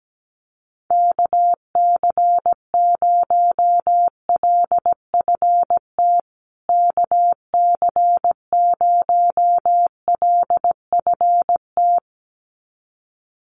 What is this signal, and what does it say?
Morse code, heard as a keyed tone, about 17 wpm, 700 Hz. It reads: KC0LFT KC0LFT
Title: USA Call Signs Repeated 2x 17wpm